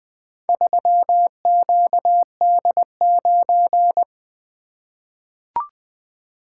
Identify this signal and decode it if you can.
Morse code, heard as a keyed tone, about 20 wpm, 700 Hz. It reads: 3QD9
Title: Four Letter-Number Combinations - 20wpm